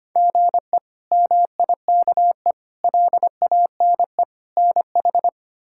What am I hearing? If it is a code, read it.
Morse code, heard as a keyed tone, about 25 wpm, 700 Hz. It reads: GE MIKE LANE NH
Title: K1USN SST QSO Practice - 25wpm